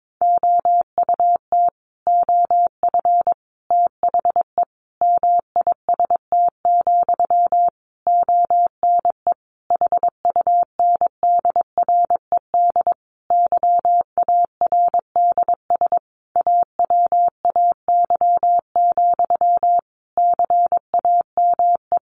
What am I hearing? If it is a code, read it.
Morse code, heard as a keyed tone, about 22 wpm, 700 Hz. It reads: OUT OF THE MIST, ONE HUNDRED YARDS AWAY, CAME
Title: A Sound Of Thunder 22wpm